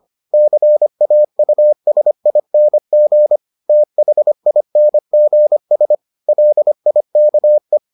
Morse code, heard as a keyed tone, about 25 wpm, 600 Hz. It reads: CAUSING THINGS LIKE